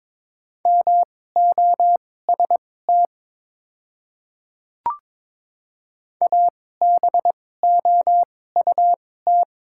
Morse code, heard as a keyed tone, about 23 wpm, spaced as for 16 wpm, 700 Hz. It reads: MOST ABOUT
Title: 2x Character Spacing - Top 100 Words - 22wpm